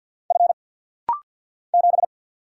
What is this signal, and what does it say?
Morse code, heard as a keyed tone, about 50 wpm, 700 Hz. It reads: F 7
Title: Single Letter-Number - Rapid-Fire 50wpm